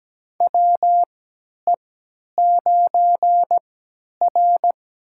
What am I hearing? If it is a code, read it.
Morse code, heard as a keyed tone, about 17 wpm, spaced as for 10 wpm, 700 Hz. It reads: WE9R
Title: USA Call Signs - 3x Character Spacing - 17wpm